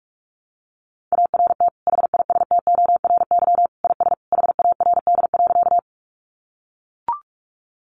Morse code, heard as a keyed tone, about 44 wpm, 700 Hz. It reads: ART HISTORY IS HARD.
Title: Sentences from Top 600 Words 45wpm